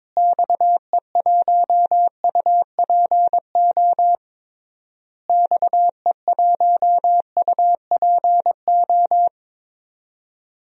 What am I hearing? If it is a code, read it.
Morse code, heard as a keyed tone, about 22 wpm, 700 Hz. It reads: XE1UPO XE1UPO
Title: DXCC-100 Call Signs - 6 Characters - Repeated 3x - 22wpm